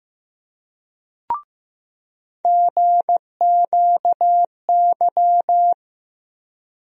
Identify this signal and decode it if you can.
Morse code, heard as a keyed tone, about 15 wpm, 700 Hz. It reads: GQY
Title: USA Call Sign Postfixes - 3 Letters 15wpm